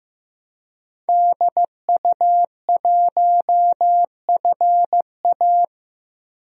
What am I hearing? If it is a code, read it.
Morse code, heard as a keyed tone, about 15 wpm, 700 Hz. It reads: DU1FA
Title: DXCC-100 Call Signs - 5 Characters - 15wpm